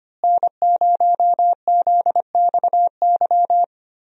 Morse code, heard as a keyed tone, about 25 wpm, 700 Hz. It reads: N0ZXY
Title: USA Call Signs - 5 Characters - Repeated 3x - 25wpm